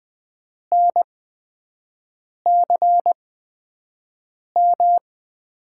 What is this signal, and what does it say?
Morse code, heard as a keyed tone, about 20 wpm, spaced as for 5 wpm, 700 Hz. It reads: NCM
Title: USA Call Signs - 8x Character Spacing - 20wpm